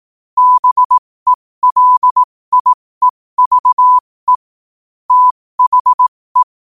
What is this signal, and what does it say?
Morse code, heard as a keyed tone, about 18 wpm, spaced as for 15 wpm, 1000 Hz. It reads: BELIEVE THE
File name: qotdfiles / 15wpm